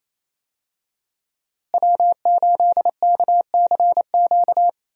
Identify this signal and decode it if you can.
Morse code, heard as a keyed tone, about 28 wpm, 700 Hz. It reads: W8KCQ
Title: USA Call Signs - 5 Characters - Repeated 2x - 28wpm